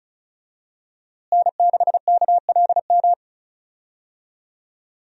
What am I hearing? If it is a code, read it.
Morse code, heard as a keyed tone, about 36 wpm, 700 Hz. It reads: N6KLM
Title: USA Call Signs - 5 Characters - Repeated 2x - 35wpm